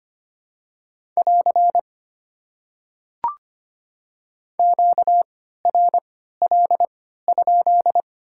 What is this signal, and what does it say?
Morse code, heard as a keyed tone, about 25 wpm, spaced as for 14 wpm, 700 Hz. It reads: <AR> QRL?
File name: Top 90 QSO Elements - 3x Character Spacing - 25wpm